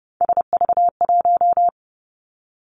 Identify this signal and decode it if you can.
Morse code, heard as a keyed tone, about 30 wpm, 700 Hz. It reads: SV1
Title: Callsign Prefixes for Top 50 DXCC - 30wpm